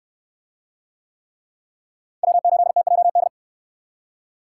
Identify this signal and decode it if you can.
Morse code, heard as a keyed tone, about 68 wpm, 700 Hz. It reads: W8TJD